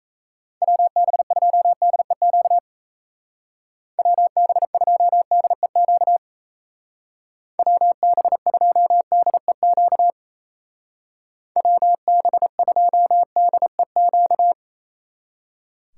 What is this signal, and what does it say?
Morse code, heard as a keyed tone, about 37 wpm, 700 Hz. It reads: WB2DEQ WB2DEQ WB2DEQ WB2DEQ